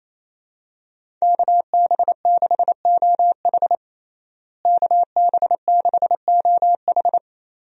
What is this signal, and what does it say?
Morse code, heard as a keyed tone, about 28 wpm, 700 Hz. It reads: KB6OH KB6OH
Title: USA Call Signs - 5 Characters - Repeated 2x - 28wpm